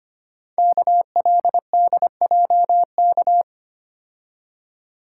Morse code, heard as a keyed tone, about 25 wpm, 700 Hz. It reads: KLDJK
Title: Code Groups - 5 Letters - 25wpm